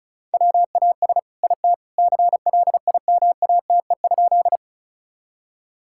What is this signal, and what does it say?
Morse code, heard as a keyed tone, about 34 wpm, 700 Hz. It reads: WAS IT CLIMATE?
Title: Sentences from Top 1600 Words 35wpm